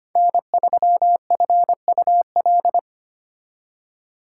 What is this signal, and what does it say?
Morse code, heard as a keyed tone, about 30 wpm, 700 Hz. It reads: N3FUL
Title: Speed Racing - USA 1x3 Call Signs 25wpm